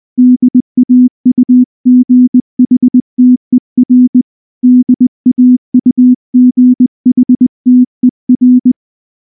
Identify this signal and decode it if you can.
Morse code, heard as a keyed tone, about 20 wpm, 250 Hz. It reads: DAUGHTER DAUGHTER
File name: morse-training-20WPM